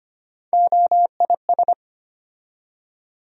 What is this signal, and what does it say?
Morse code, heard as a keyed tone, about 25 wpm, 700 Hz. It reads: OIS
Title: USA Call Sign Postfixes - 3 Letters 25wpm